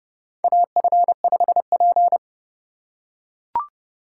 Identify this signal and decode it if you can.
Morse code, heard as a keyed tone, about 31 wpm, 700 Hz. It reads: AF5P
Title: K1USN SST QSO Practice - 30wpm